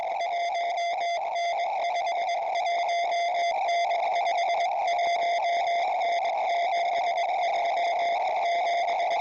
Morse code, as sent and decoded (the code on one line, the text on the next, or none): none